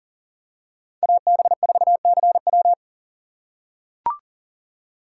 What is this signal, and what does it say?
Morse code, heard as a keyed tone, about 40 wpm, 700 Hz. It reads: AB4CW
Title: USA Call Signs - 5 Characters - 4x Character Spacing - 40wpm